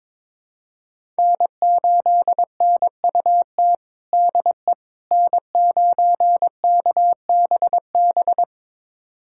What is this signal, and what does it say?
Morse code, heard as a keyed tone, about 22 wpm, 700 Hz. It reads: N8NUT DE N9KBB
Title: USA Callsign DE Callsign - 5 Characters - 22wpm